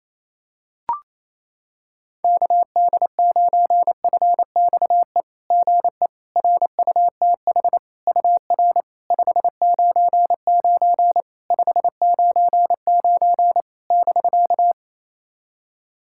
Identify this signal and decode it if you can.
Morse code, heard as a keyed tone, about 28 wpm, 690 Hz. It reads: KD9FXE GE RUTH UR 599 599 <BK>
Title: SOTA QSO Practice 28wpm